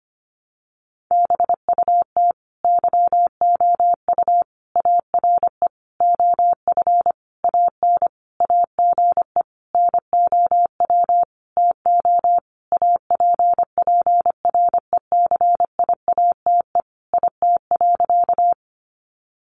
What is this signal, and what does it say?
Morse code, heard as a keyed tone, about 25 wpm, 700 Hz. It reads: BUT YOU ARE OF AN AGE NOW TO APPRECIATE IT.
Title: Sentences from Top 1300 Words 25wpm